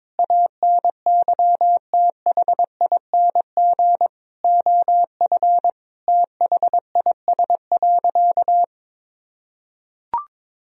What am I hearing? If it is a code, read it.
Morse code, heard as a keyed tone, about 22 wpm, 700 Hz. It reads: ANYTHING OF THIS.
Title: Sentences from Top 1000 Words - Review Part 3 - 22wpm